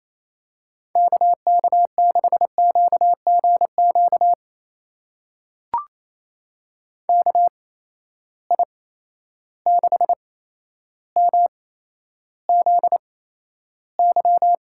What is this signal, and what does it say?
Morse code, heard as a keyed tone, about 28 wpm, spaced as for 12 wpm, 700 Hz. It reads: KK6QGQ KI6MZY
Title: USA Call Signs - 6 Characters - 8x Character Spacing - 28wpm